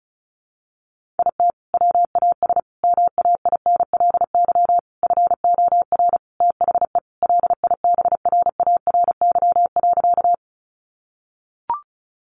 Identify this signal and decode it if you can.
Morse code, heard as a keyed tone, about 34 wpm, 700 Hz. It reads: IT WAS MAINLY FOR THE LIBRARY.
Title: Sentences from Top 1400 Words 35wpm